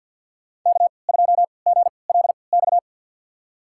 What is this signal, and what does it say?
Morse code, heard as a keyed tone, about 47 wpm, spaced as for 28 wpm, 700 Hz. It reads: K2GLX
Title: USA Call Signs - 5 Characters - 3x Character Spacing - 50wpm